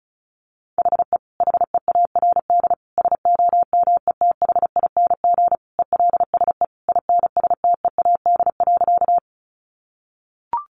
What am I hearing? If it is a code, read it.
Morse code, heard as a keyed tone, about 34 wpm, 700 Hz. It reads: HE HEARD SOMETHING ELSE INSTEAD.
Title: Sentences from Top 800 Words 35wpm